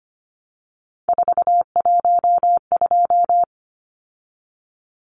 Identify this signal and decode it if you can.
Morse code, heard as a keyed tone, about 25 wpm, 700 Hz. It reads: 412
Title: Numbers - 3-Digit 25wpm